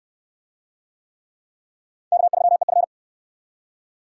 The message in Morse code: K3EF

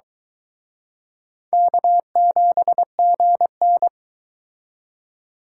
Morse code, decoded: K7GN

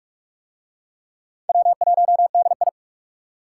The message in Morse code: W1DI

